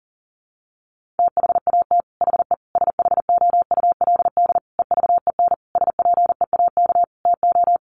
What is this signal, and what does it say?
Morse code, heard as a keyed tone, about 40 wpm, 700 Hz. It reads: THAT HE SHOULD EVEN SPEAK TO
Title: Sentences from Top 1500 Words 40wpm